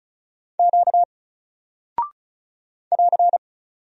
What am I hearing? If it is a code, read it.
Morse code, heard as a keyed tone, about 36 wpm, 700 Hz. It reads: Q <AR>
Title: Single Character (Letter:Number:Punctuation:Prosign) Alt 35wpm